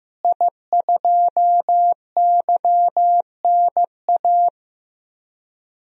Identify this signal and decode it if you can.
Morse code, heard as a keyed tone, about 15 wpm, 700 Hz. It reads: I2YNA
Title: DXCC-100 Call Signs - 5 Characters - Repeated 2x - 15wpm